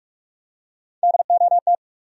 Morse code, heard as a keyed tone, about 47 wpm, 700 Hz. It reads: DOT
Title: USA Common Names - Max 3 Letters - 45wpm